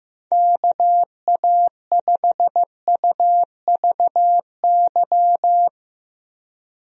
Morse code, heard as a keyed tone, about 15 wpm, 700 Hz. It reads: KA5UVY
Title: USA Call Signs - 6 Characters - Repeated 2x - 15wpm